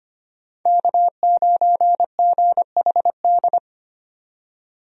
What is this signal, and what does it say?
Morse code, heard as a keyed tone, about 25 wpm, 700 Hz. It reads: K9GHD
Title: USA Call Signs - 5 Characters - 25wpm